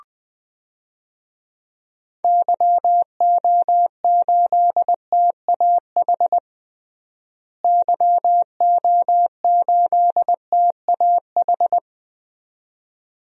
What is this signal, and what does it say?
Morse code, heard as a keyed tone, about 20 wpm, 700 Hz. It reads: YO8TAH YO8TAH
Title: DXCC-100 Call Signs - 6 Characters - Repeated 2x - 20wpm